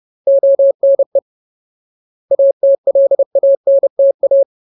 Morse code, heard as a keyed tone, about 30 wpm, 550 Hz. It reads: ONE ATLANTA
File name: Ultimate Word List / 30WPM Ultimate